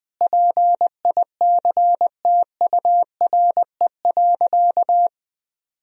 Morse code, heard as a keyed tone, about 20 wpm, 700 Hz. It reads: PICTURE.
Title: Sentences from Top 1000 Words - Review Part 1 - 20wpm